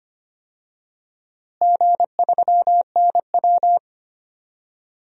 Morse code, heard as a keyed tone, about 25 wpm, 700 Hz. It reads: G3NW
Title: DXCC-100 Call Signs - 4 Characters - Repeated 3x - 25wpm